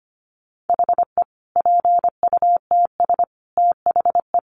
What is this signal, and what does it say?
Morse code, heard as a keyed tone, about 25 wpm, 700 Hz. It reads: HE PUTS THE
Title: Additional Easy Phrases 25wpm